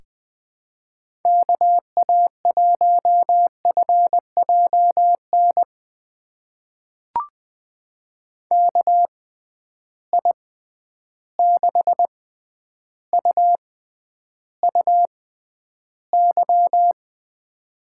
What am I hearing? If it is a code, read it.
Morse code, heard as a keyed tone, about 20 wpm, spaced as for 10 wpm, 700 Hz. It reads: KA1FJN KI6UUY
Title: USA Call Signs - 6 Characters - 6x Character Spacing - 20wpm